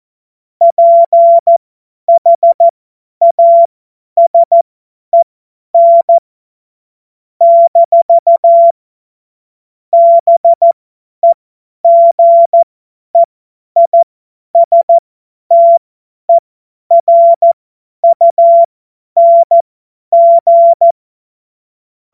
Morse code, heard as a keyed tone, about 14 wpm, spaced as for 10 wpm, 680 Hz. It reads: PHASEN - BEGEISTERUNG
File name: Morsekurs / Witze / Witz 09. Dezember 2025 50 F70